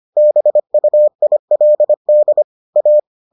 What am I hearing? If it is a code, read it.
Morse code, heard as a keyed tone, about 25 wpm, 600 Hz. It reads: BUILD A